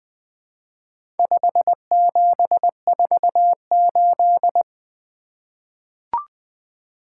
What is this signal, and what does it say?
Morse code, heard as a keyed tone, about 20 wpm, 700 Hz. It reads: 5748
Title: Four Letter-Number Combinations - 20wpm